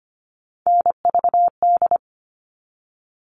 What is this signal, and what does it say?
Morse code, heard as a keyed tone, about 25 wpm, 700 Hz. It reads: NVD
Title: All Cut Numbers - 3-Digit 25wpm